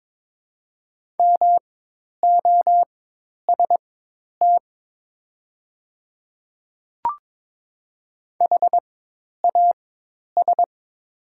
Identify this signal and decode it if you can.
Morse code, heard as a keyed tone, about 22 wpm, spaced as for 10 wpm, 700 Hz. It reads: MOST HAS
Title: 4x Character Spacing - Top 100 Words - 22wpm